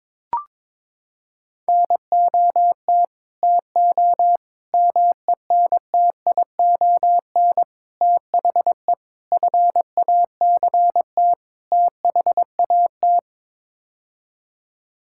Morse code, heard as a keyed tone, about 22 wpm, 700 Hz. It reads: NOT TO MENTION THE FACT THAT
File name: practice / 22wpm / Sets of 6 Words - 22wpm